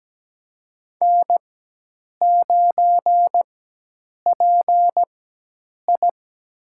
Morse code, heard as a keyed tone, about 17 wpm, spaced as for 8 wpm, 700 Hz. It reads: N9PI
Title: USA Call Signs - 4 Characters - 4x Character Spacing - 17wpm